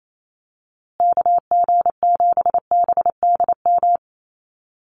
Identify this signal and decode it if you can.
Morse code, heard as a keyed tone, about 28 wpm, 700 Hz. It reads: KG7BDM
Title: USA Call Signs Repeated 2x 28wpm